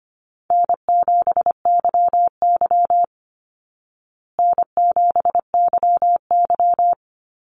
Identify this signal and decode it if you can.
Morse code, heard as a keyed tone, about 25 wpm, 700 Hz. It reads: N7YY N7YY
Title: Easy Call Signs Repeated 2x - 25wpm